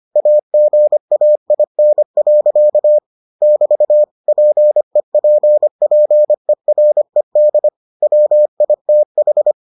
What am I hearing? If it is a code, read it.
Morse code, heard as a keyed tone, about 25 wpm, 600 Hz. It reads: AGAIN. <BT> PEPPERED WITH